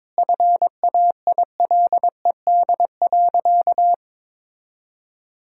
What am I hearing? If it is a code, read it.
Morse code, heard as a keyed tone, about 22 wpm, 700 Hz. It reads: FAILED.
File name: Sentences with 9-Word Phrases - 22wpm